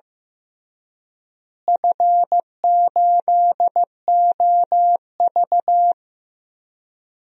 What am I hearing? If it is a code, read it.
Morse code, heard as a keyed tone, about 15 wpm, 700 Hz. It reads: F8OV
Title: DXCC-100 Call Signs - 4 Characters - Repeated 3x - 15wpm